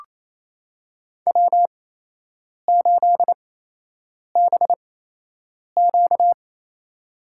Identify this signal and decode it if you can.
Morse code, heard as a keyed tone, about 28 wpm, spaced as for 8 wpm, 700 Hz. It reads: W8BQ
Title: USA Call Signs - 8x Character Spacing - 28wpm